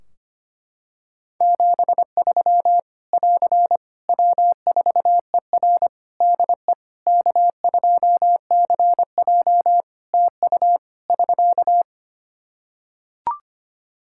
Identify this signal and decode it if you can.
Morse code, heard as a keyed tone, about 25 wpm, 700 Hz. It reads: 73 <AR> W4ER DE K2CJ TU <SK>
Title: Practice QSO - Exchange 4 - Ending - 25wpm